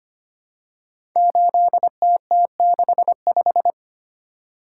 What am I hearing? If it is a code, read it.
Morse code, heard as a keyed tone, about 25 wpm, 700 Hz. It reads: 8TT65
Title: Code Groups - 5 Letter:Numbers - 25wpm